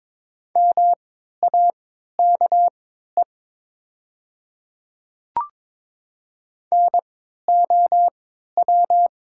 Morse code, heard as a keyed tone, about 22 wpm, spaced as for 13 wpm, 700 Hz. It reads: MAKE NOW
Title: 3x Character Spacing - Top 300 Words - 22wpm